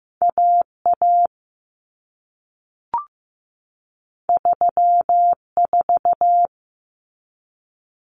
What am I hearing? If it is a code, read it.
Morse code, heard as a keyed tone, about 15 wpm, 700 Hz. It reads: AA 34
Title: Common Cut Numbers - 2-Digit 15wpm